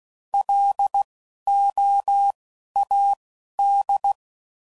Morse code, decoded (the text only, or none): LOAD